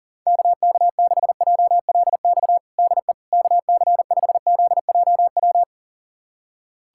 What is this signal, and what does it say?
Morse code, heard as a keyed tone, about 40 wpm, 700 Hz. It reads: KK6JLX DE KC5ZJW
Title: USA Callsign DE Callsign 40wpm